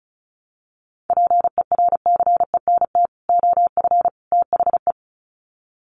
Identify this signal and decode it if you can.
Morse code, heard as a keyed tone, about 34 wpm, 700 Hz. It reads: PERCENT OF THE
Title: Sets of 3 Words 35wpm